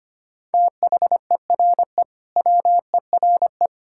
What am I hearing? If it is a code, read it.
Morse code, heard as a keyed tone, about 25 wpm, 700 Hz. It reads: THERE WERE
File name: practice / 25wpm / Sentences with 2-Word Phrases - 25wpm